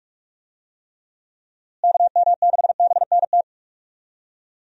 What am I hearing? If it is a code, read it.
Morse code, heard as a keyed tone, about 46 wpm, 700 Hz. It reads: KM6BNT
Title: USA Call Signs - 6 Characters - Repeated 2x - 45wpm